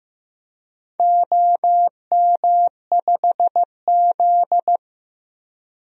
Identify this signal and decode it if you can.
Morse code, heard as a keyed tone, about 15 wpm, 700 Hz. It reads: OM5Z
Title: DXCC-100 Call Signs - 4 Characters - Repeated 3x - 15wpm